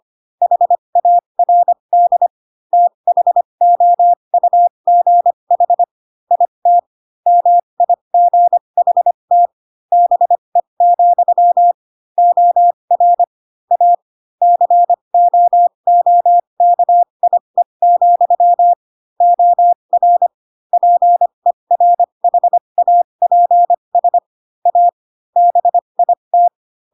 Morse code, heard as a keyed tone, about 25 wpm, spaced as for 22 wpm, 700 Hz. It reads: HARD THOUGH IT MIGHT BE, OR A COOKIE, OR PERHAPS A BIT